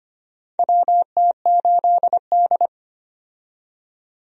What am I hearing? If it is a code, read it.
Morse code, heard as a keyed tone, about 25 wpm, 700 Hz. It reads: WT8D
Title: USA Call Signs - 4 Characters - 25wpm